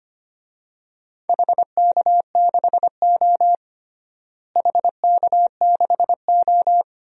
Morse code, heard as a keyed tone, about 25 wpm, 700 Hz. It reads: HK6O HK6O
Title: DXCC-100 Call Signs - 4 Characters - Repeated 3x - 25wpm